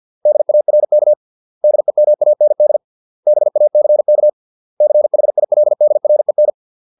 Morse code, heard as a keyed tone, about 50 wpm, 600 Hz. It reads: DARK DEMAND BACK CHILDREN